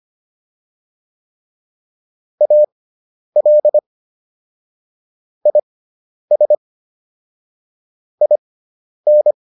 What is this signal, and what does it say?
Morse code, heard as a keyed tone, about 25 wpm, spaced as for 10 wpm, 600 Hz. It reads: AL IS IN